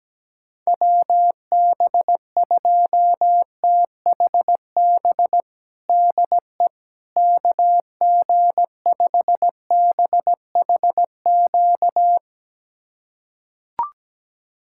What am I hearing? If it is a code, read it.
Morse code, heard as a keyed tone, about 17 wpm, 700 Hz. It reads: WB2THB DE KG5BHQ
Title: USA Callsign DE Callsign 17wpm